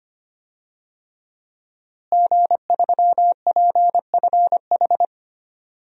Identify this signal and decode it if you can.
Morse code, heard as a keyed tone, about 25 wpm, 700 Hz. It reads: G3PFH
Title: DXCC-100 Call Signs - 5 Characters - Repeated 3x - 25wpm